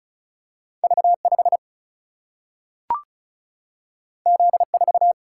Morse code, heard as a keyed tone, about 36 wpm, 700 Hz. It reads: V5 Z4
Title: Single-Letter Single-Number 35wpm